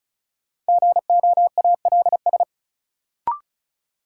Morse code, heard as a keyed tone, about 36 wpm, 700 Hz. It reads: GOALS
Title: Sentences from Top 1000 Words - Review Part 6 - New Word Drill - 35wpm